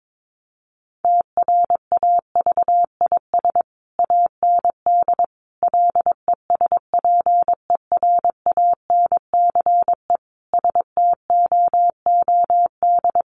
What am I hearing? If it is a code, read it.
Morse code, heard as a keyed tone, about 22 wpm, 700 Hz. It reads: TRAVIS AND LESPERANCE STOOD
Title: A Sound Of Thunder 22wpm